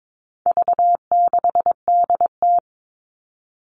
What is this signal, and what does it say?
Morse code, heard as a keyed tone, about 22 wpm, 700 Hz. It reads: V6DT
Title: All Cut Numbers - 4-Digit 22wpm